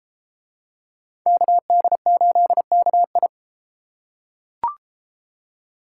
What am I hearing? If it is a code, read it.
Morse code, heard as a keyed tone, about 26 wpm, 700 Hz. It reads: KD8KI
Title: Speed Racing - USA 2x2 Call Signs 22wpm